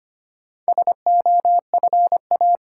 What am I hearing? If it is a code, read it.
Morse code, heard as a keyed tone, about 25 wpm, 700 Hz. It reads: SOFA
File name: practice / 25wpm / Easy Words with Hints - 25wpm